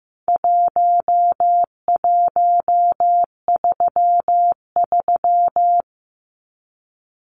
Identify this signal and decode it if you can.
Morse code, heard as a keyed tone, about 15 wpm, 700 Hz. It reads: 1133
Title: Numbers - 4-Digit 15wpm